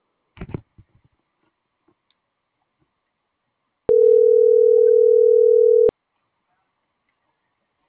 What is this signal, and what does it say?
Ringback tone (North America), 440 + 480 Hz.